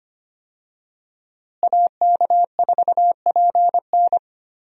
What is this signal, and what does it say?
Morse code, heard as a keyed tone, about 25 wpm, 700 Hz. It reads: AK4PN
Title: USA Call Signs - 5 Characters - Repeated 2x - 25wpm